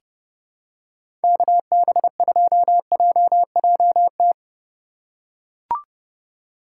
Morse code, heard as a keyed tone, about 30 wpm, 700 Hz. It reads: KB2JJT
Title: USA Call Signs - 6 Characters - 2x Character Spacing - 30wpm